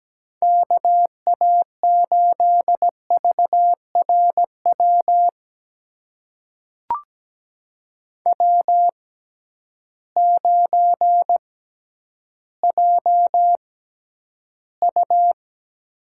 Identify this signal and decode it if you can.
Morse code, heard as a keyed tone, about 17 wpm, spaced as for 10 wpm, 700 Hz. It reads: KA8VRW W9JU
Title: USA Call Signs - 6x Character Spacing - 17wpm